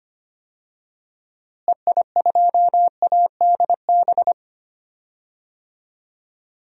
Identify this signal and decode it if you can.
Morse code, heard as a keyed tone, about 25 wpm, 700 Hz. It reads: EI2ADB
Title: DXCC-100 Call Signs - 6 Characters - Repeated 3x - 25wpm